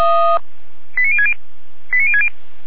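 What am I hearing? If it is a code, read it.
Five-tone selective call (ZVEI), heard as quick runs of five tones, tones of about 70 ms. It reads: 89060 89060
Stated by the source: ZVEI five-tone selective call 89099 received by SDR, Feuerwehr Orlamünde audio server.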